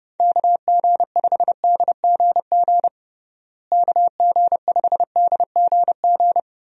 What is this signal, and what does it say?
Morse code, heard as a keyed tone, about 32 wpm, 700 Hz. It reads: KG5DGG KG5DGG
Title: USA Call Signs - 6 Characters - Repeated 3x - 30wpm